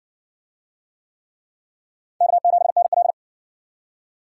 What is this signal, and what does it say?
Morse code, heard as a keyed tone, about 54 wpm, 700 Hz. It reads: K7NL